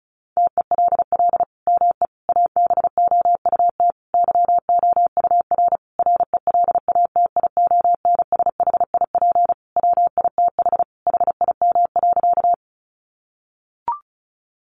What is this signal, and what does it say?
Morse code, heard as a keyed tone, about 34 wpm, 700 Hz. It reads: TELL ME ABOUT YOUR RELATIONSHIP WITH HIM.
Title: Sentences from Top 1100 Words 35wpm